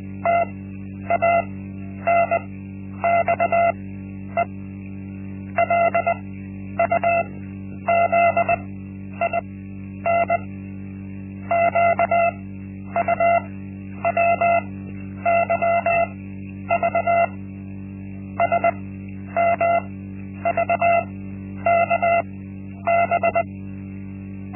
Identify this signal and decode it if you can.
Morse code, heard as a keyed tone, about 20 wpm, spaced as for 10 wpm, 680 Hz. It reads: TANXE LUZIN QUWYV SMVKB
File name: Linksmakalnis-1